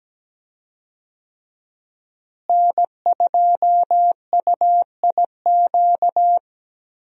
Morse code, heard as a keyed tone, about 17 wpm, 700 Hz. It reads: N2UIQ
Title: USA Call Signs - 5 Characters - Repeated 2x - 17wpm